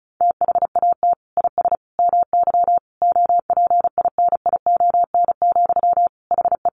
Morse code, heard as a keyed tone, about 34 wpm, 700 Hz. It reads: THAT IS MY OPINION, HE
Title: Sentences from Top 1200 Words 35wpm